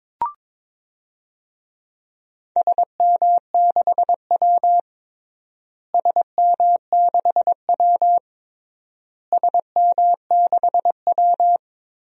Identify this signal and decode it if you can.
Morse code, heard as a keyed tone, about 22 wpm, 700 Hz. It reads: SM6W SM6W SM6W
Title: DXCC-100 Call Signs - 4 Characters - Repeated 3x - 22wpm